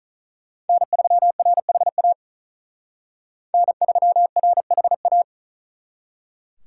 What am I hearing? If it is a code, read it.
Morse code, heard as a keyed tone, about 46 wpm, 700 Hz. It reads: N3RHA N3RHA